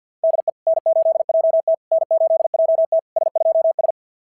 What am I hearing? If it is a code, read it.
Morse code, heard as a keyed tone, about 49 wpm, 690 Hz. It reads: DE N8JT N8JT S2S
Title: SOTA QSO Practice 50wpm